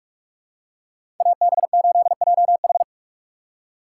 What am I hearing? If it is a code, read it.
Morse code, heard as a keyed tone, about 45 wpm, 700 Hz. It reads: AB8JH